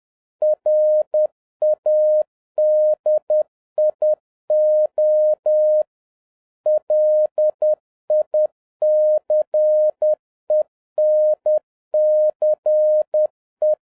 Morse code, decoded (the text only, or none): RADIO LICENCE